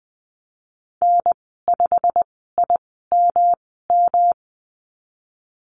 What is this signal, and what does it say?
Morse code, heard as a keyed tone, about 20 wpm, spaced as for 14 wpm, 700 Hz. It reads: N5IMM
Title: 128 Lesson 21g - Number 6 - Call Signs 20wpm